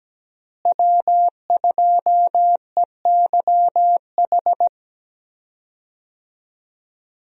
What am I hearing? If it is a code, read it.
Morse code, heard as a keyed tone, about 17 wpm, 700 Hz. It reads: W2EYH
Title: USA Call Signs - 5 Characters - Repeated 2x - 17wpm